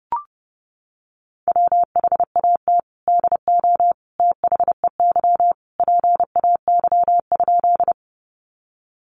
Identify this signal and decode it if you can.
Morse code, heard as a keyed tone, about 30 wpm, 700 Hz. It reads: WHAT DO THEY PAY?
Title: Sentences from Top 700 Words 30wpm